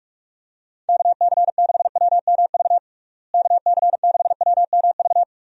Morse code, heard as a keyed tone, about 46 wpm, 700 Hz. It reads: KC6WMV KC6WMV
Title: USA Call Signs - 6 Characters - Repeated 3x - 45wpm